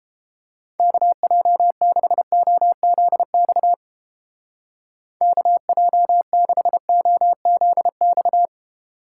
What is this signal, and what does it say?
Morse code, heard as a keyed tone, about 30 wpm, 700 Hz. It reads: KJ6OZX KJ6OZX